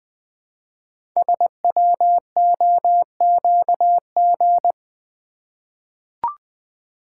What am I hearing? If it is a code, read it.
Morse code, heard as a keyed tone, about 20 wpm, 700 Hz. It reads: SWOQG
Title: Code Groups - 5 Letters - 20wpm